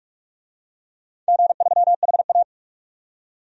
Morse code, heard as a keyed tone, about 43 wpm, 700 Hz. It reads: G3HA
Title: DXCC-100 Call Signs - 4 Characters - 45wpm